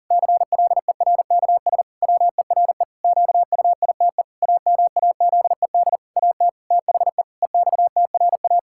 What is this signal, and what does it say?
Morse code, heard as a keyed tone, about 40 wpm, 700 Hz. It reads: CLERKS WERE QUITE AMAZED AT THE EXTRA
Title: Pangram Challenge v2 - 40wpm